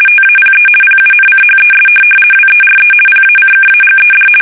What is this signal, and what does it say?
RTTY, about 50 baud. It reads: TAPE PUNCHES WERE PACKAGED
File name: fsk850-50B / teletype-corp